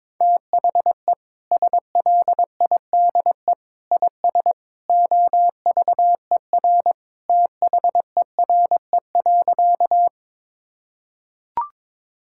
Morse code, heard as a keyed tone, about 22 wpm, 700 Hz. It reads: THE SLIDE IS OVER THERE.
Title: Sentences from Top 1600 Words 22wpm